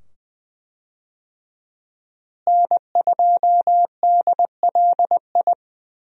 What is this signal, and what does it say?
Morse code, heard as a keyed tone, about 20 wpm, 700 Hz. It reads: N2DLI
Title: USA Call Signs - 5 Characters - Repeated 2x - 20wpm